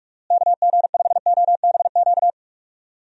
Morse code, heard as a keyed tone, about 41 wpm, 700 Hz. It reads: KG5OBQ
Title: USA Call Signs - 6 Characters - Repeated 2x - 45wpm